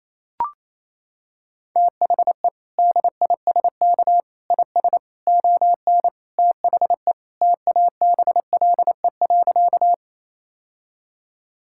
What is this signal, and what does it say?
Morse code, heard as a keyed tone, about 28 wpm, 700 Hz. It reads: THE DISK IS ON THE TABLE.
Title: Sentences from Top 1000 Words - Review Part 6 - 28wpm